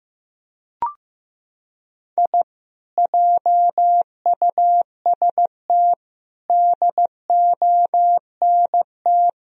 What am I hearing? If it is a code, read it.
Morse code, heard as a keyed tone, about 15 wpm, 700 Hz. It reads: I JUST DONT
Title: Sentences from Top 1000 Words - Review Part 2 - 15wpm